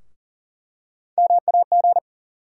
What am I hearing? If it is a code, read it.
Morse code, heard as a keyed tone, about 40 wpm, 700 Hz. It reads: MAG
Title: USA Call Sign Postfixes - 3 Letters 40wpm